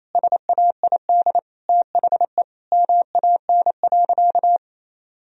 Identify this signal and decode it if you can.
Morse code, heard as a keyed tone, about 28 wpm, 700 Hz. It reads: SAID THE MAN.
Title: Sentences from Top 1000 Words - Review Part 3 - 28wpm